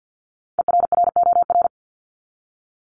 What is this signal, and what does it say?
Morse code, heard as a keyed tone, about 48 wpm, 700 Hz. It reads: ERROR
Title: Top 1751-2000 Words 50wpm